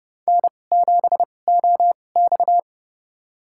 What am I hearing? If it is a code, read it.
Morse code, heard as a keyed tone, about 32 wpm, spaced as for 22 wpm, 700 Hz. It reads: N7OX
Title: USA Call Signs - 4 Characters - 2x Character Spacing - 30wpm